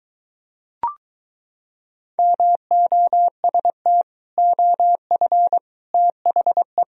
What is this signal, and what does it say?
Morse code, heard as a keyed tone, about 23 wpm, 700 Hz. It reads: MOST OF THE